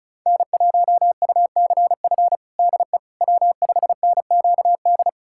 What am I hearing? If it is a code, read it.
Morse code, heard as a keyed tone, about 36 wpm, 700 Hz. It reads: N1UCF DE W5NQD
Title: USA Callsign DE Callsign - 5 Characters - 35wpm